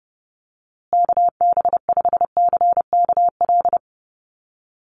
Morse code, heard as a keyed tone, about 30 wpm, 700 Hz. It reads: KB5CKL
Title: USA Call Signs Repeated 3x 30wpm